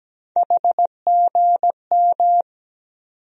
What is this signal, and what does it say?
Morse code, heard as a keyed tone, about 17 wpm, 700 Hz. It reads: HGM
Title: USA Call Sign Postfixes - 3 Letters 17wpm